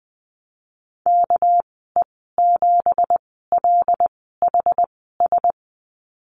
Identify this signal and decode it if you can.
Morse code, heard as a keyed tone, about 19 wpm, spaced as for 14 wpm, 700 Hz. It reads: KE7LHS
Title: 231 Lesson 37g - Letter x - Call Signs 20wpm